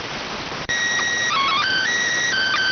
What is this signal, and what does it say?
SSTV (slow-scan television), PD 290.